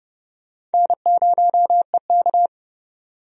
Morse code, heard as a keyed tone, about 30 wpm, 700 Hz. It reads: N0EK
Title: USA Call Signs - 4 Characters - 30wpm